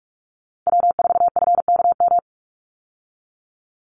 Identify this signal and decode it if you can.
Morse code, heard as a keyed tone, about 44 wpm, 700 Hz. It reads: W4FKM
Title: Call Signs Repeated 3x - 45wpm